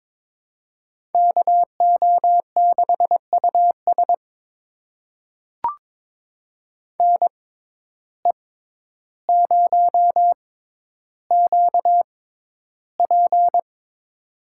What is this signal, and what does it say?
Morse code, heard as a keyed tone, about 22 wpm, spaced as for 11 wpm, 700 Hz. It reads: KO6US NE0QP
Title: USA Call Signs - 6x Character Spacing - 22wpm